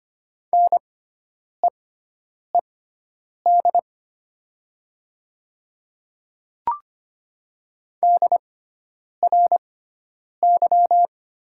Morse code, heard as a keyed tone, about 25 wpm, spaced as for 9 wpm, 700 Hz. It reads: NEED DRY
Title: 6x Character Spacing - Top 500 Words - 25wpm